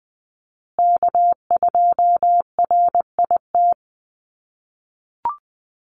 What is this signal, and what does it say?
Morse code, heard as a keyed tone, about 20 wpm, 700 Hz. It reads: K2RIT
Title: USA Call Signs Repeated 2x 20wpm